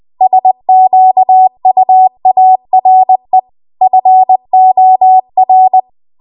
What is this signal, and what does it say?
Morse code, heard as a keyed tone, about 20 wpm, 750 Hz. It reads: SQUARE FOR
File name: Archive / 20 WPM / 20WPM